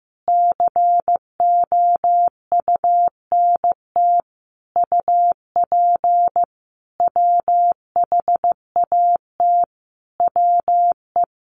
Morse code, heard as a keyed tone, about 15 wpm, 700 Hz. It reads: COUNT UP WHAT WE
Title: Sentences from Top 1500 Words 15wpm